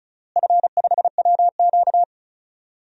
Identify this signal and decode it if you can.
Morse code, heard as a keyed tone, about 36 wpm, 700 Hz. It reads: F5WQ
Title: DXCC-100 Call Signs - 4 Characters - Repeated 3x - 35wpm